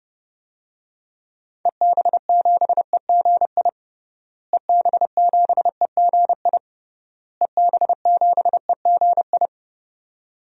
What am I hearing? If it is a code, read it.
Morse code, heard as a keyed tone, about 30 wpm, 700 Hz. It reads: EB7EGI EB7EGI EB7EGI
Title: DXCC-100 Call Signs - 6 Characters - Repeated 3x - 30wpm